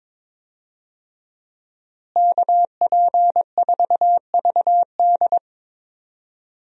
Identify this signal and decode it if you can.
Morse code, heard as a keyed tone, about 22 wpm, 700 Hz. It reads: KP4VD
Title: DXCC-100 Call Signs - 5 Characters - Repeated 3x - 22wpm